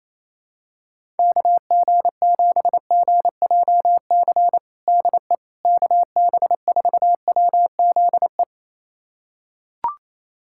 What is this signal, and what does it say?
Morse code, heard as a keyed tone, about 28 wpm, 700 Hz. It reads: KG7GJC DE KB4WZE
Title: USA Callsign DE Callsign 28wpm